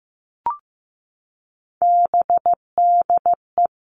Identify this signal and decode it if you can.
Morse code, heard as a keyed tone, about 15 wpm, 700 Hz. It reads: BDE
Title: All Cut Numbers - 3-Digit 15wpm